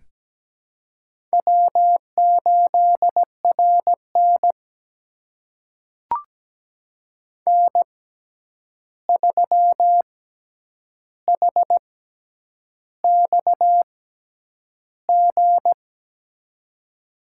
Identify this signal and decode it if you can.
Morse code, heard as a keyed tone, about 17 wpm, spaced as for 8 wpm, 700 Hz. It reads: W8RN N3HXG
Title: USA Call Signs - 6x Character Spacing - 17wpm